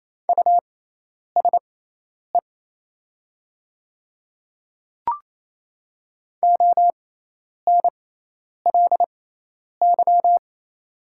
Morse code, heard as a keyed tone, about 28 wpm, spaced as for 10 wpm, 700 Hz. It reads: USE ONLY
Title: 6x Character Spacing - Top 200 Words - 28wpm